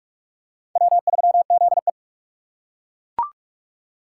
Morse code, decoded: W3ZE

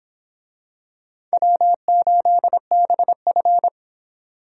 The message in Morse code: W8BF

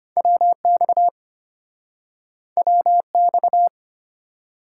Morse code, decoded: WX WX

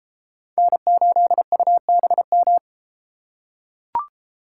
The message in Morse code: N8UBM

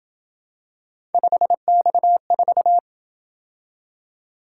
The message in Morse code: 5X4

